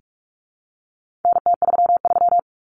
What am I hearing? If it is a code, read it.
Morse code, heard as a keyed tone, about 44 wpm, 700 Hz. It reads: NT33
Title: Common Cut Numbers - 4-Digit 45wpm